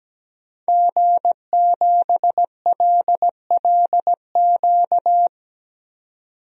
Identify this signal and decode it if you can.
Morse code, heard as a keyed tone, about 17 wpm, 700 Hz. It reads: G7LLQ
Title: DXCC-100 Call Signs - 5 Characters - 17wpm